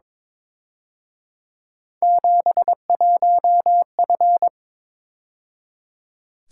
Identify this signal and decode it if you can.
Morse code, heard as a keyed tone, about 26 wpm, 700 Hz. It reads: 71F
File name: Speed-Racing - Three Letter-Number Combinations - 22wpm